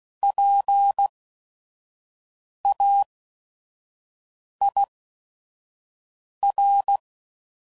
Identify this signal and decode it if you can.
Morse code, heard as a keyed tone, about 16 wpm, spaced as for 5 wpm, 800 Hz. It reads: PAIR
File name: Prefixes / By First Letter / Ps